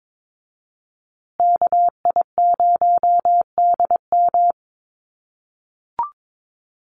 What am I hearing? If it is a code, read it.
Morse code, heard as a keyed tone, about 22 wpm, 700 Hz. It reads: KI0DM
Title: USA Call Signs Repeated 3x 22wpm